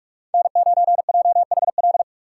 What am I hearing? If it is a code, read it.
Morse code, heard as a keyed tone, about 46 wpm, 700 Hz. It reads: N9JHL
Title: K1USN SST QSO Practice - 45wpm